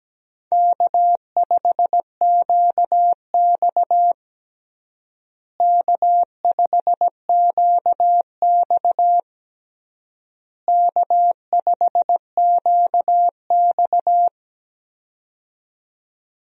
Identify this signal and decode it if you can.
Morse code, heard as a keyed tone, about 17 wpm, 700 Hz. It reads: K5QX K5QX K5QX
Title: USA Call Signs - 4 Characters - Repeated 3x - 17wpm